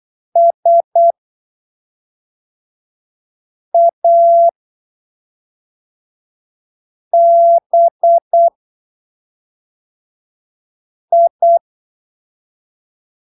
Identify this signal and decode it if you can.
Morse code, heard as a keyed tone, about 8 wpm, spaced as for 3 wpm, 680 Hz. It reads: SABI